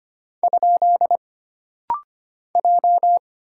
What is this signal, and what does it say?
Morse code, heard as a keyed tone, about 25 wpm, 700 Hz. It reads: ? J
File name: Single Character (Letter:Number:Punctuation:Prosign) - Rapid-Fire - 25wpm